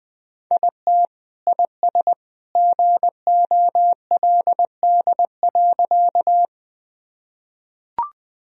Morse code, heard as a keyed tone, about 20 wpm, 700 Hz. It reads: IT IS GOLD.
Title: Sentences from Top 1000 Words - Review Part 3 - 20wpm